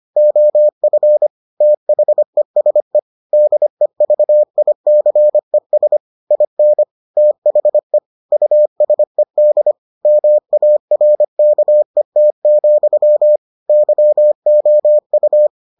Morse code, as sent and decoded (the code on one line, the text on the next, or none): --- ..-. / - .... ... / -.. ...- .. -.-. ... / .. -. / - .... / ..- ... -.. / -- .- .-. -.- . - --..-- / -.-- --- ..-
OF THESE DEVICES IN THE USED MARKET, YOU